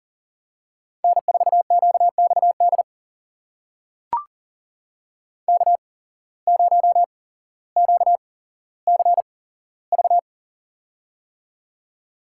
Morse code, heard as a keyed tone, about 40 wpm, spaced as for 17 wpm, 700 Hz. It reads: N4QXD K0QCV
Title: USA Call Signs - 8x Character Spacing - 40wpm